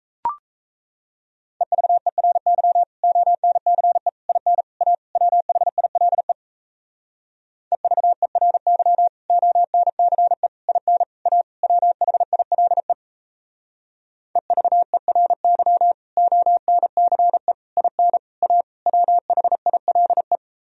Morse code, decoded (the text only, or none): EVERY ONCE IN A WHILE EVERY ONCE IN A WHILE EVERY ONCE IN A WHILE